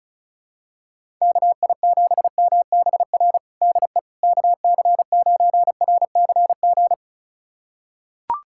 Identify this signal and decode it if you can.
Morse code, heard as a keyed tone, about 36 wpm, 700 Hz. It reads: KI7MBR DE KC9RCG
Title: USA Callsign DE Callsign - 6 Characters - 35wpm